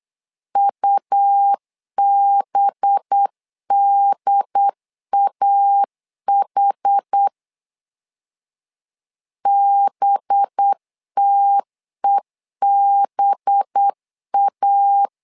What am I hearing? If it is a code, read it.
Morse code, heard as a keyed tone, about 8 wpm, 800 Hz. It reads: UBDAH BTEBA
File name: LEZIONI / 5 LBVC / 5 LB 8wpm 2816